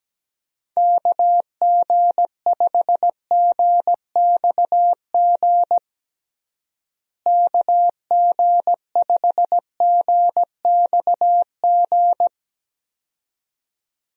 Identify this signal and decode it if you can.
Morse code, heard as a keyed tone, about 17 wpm, 700 Hz. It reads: KG5GXG KG5GXG
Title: USA Call Signs - 6 Characters - Repeated 2x - 17wpm